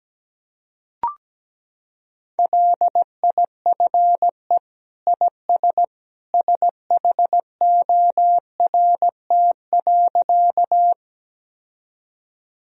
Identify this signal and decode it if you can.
Morse code, heard as a keyed tone, about 17 wpm, 700 Hz. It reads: LIFE IS SHORT.
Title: Sentences from Top 1000 Words - Review Part 2 - 17wpm